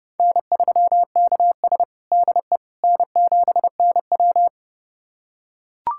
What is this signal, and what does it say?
Morse code, heard as a keyed tone, about 30 wpm, 700 Hz. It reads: N3KS DE N7NW
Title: USA Callsign DE Callsign - 4 Characters - 30wpm